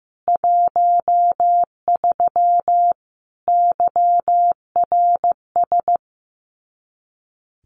Morse code, heard as a keyed tone, about 15 wpm, 700 Hz. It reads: 13 YRS
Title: Common QSO Phrases 15wpm